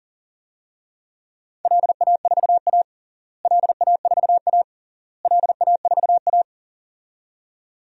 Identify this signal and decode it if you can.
Morse code, heard as a keyed tone, about 40 wpm, 700 Hz. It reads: LA4A LA4A LA4A
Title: DXCC-100 Call Signs - 4 Characters - Repeated 3x - 40wpm